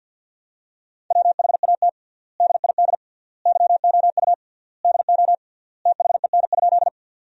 Morse code, heard as a keyed tone, about 50 wpm, 700 Hz. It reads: WHAT DID YOU DO THEN?
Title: 3x Word-Spacing - Sentences from Top 100 Words - 50wpm